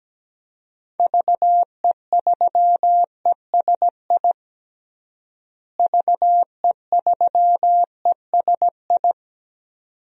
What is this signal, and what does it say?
Morse code, heard as a keyed tone, about 17 wpm, 700 Hz. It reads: VE3ESI VE3ESI
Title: DXCC-100 Call Signs - 6 Characters - Repeated 2x - 17wpm